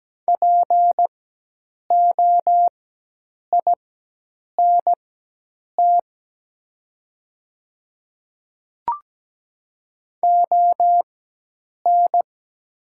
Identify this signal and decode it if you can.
Morse code, heard as a keyed tone, about 17 wpm, spaced as for 8 wpm, 700 Hz. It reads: POINT ON
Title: 4x Character Spacing - Top 200 Words - 17wpm